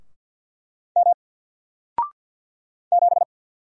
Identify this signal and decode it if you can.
Morse code, heard as a keyed tone, about 54 wpm, 700 Hz. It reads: M 7
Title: Single Letter-Number Alt 50wpm